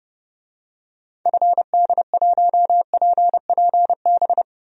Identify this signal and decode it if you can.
Morse code, heard as a keyed tone, about 30 wpm, 700 Hz. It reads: FD1PPB
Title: DXCC-100 Call Signs - 6 Characters - Repeated 3x - 30wpm